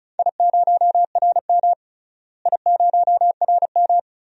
Morse code, heard as a keyed tone, about 34 wpm, 700 Hz. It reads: I0RM I0RM
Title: DXCC-100 Call Signs - 4 Characters - Repeated 2x - 35wpm